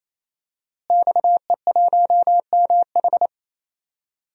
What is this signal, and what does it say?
Morse code, heard as a keyed tone, about 27 wpm, 700 Hz. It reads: XE1MH
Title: DXCC-100 Call Signs - 5 Characters - Repeated 3x - 28wpm